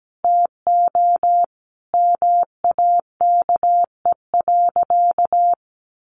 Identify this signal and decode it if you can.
Morse code, heard as a keyed tone, about 17 wpm, 700 Hz. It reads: TO MAKE.
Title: Sentences from Top 900 Words 17wpm